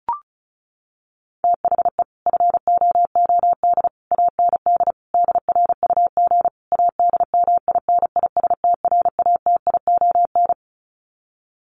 Morse code, heard as a keyed tone, about 34 wpm, 700 Hz. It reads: THE FOOD AND DRUG ADMINISTRATION
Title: Sets of 5 Words 35wpm